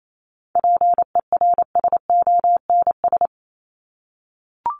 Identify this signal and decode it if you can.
Morse code, heard as a keyed tone, about 28 wpm, 700 Hz. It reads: PERSONS
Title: ES:S Suffix - Top 100 Words with Base Words 28wpm